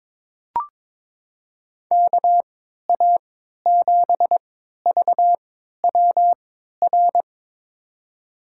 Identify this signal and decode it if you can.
Morse code, heard as a keyed tone, about 22 wpm, spaced as for 13 wpm, 700 Hz. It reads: KA7VWR
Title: USA Call Signs - 3x Character Spacing - 22wpm